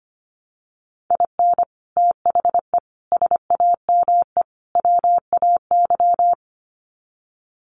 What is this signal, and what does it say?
Morse code, heard as a keyed tone, about 25 wpm, 700 Hz. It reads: IN THE SAME WAY
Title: Sets of 4 Words 25wpm